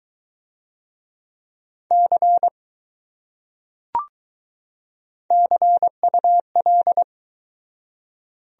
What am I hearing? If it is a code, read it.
Morse code, heard as a keyed tone, about 21 wpm, 700 Hz. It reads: C CUL